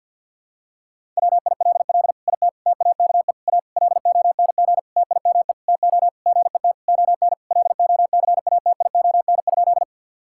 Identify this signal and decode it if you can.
Morse code, heard as a keyed tone, about 48 wpm, 700 Hz. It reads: WILL IT TAKE A LONG TIME TO GET ON LOCATION?
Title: Sentences from Top 1000 Words - Review Part 6 - 50wpm